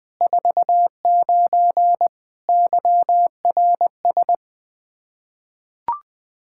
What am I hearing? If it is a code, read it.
Morse code, heard as a keyed tone, about 21 wpm, 700 Hz. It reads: 49 YRS
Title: K1USN SST QSO Practice 20wpm